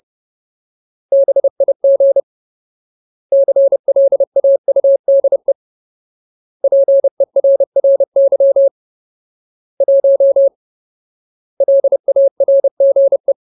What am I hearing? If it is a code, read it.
Morse code, heard as a keyed tone, about 30 wpm, 550 Hz. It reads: BIG CLAUDE PERRY 1 LARGE
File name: Ultimate Word List / 30WPM Ultimate